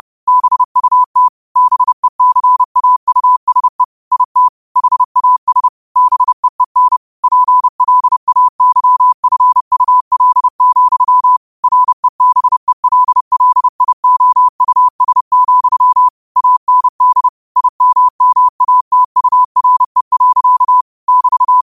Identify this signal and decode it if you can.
Morse code, heard as a keyed tone, about 30 wpm, 1000 Hz. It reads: BUT BECAUSE IT HAS BEEN PLAYFUL, REBELLIOUS, AND IMMATURE. <BT>
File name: qotdfiles / 30wpm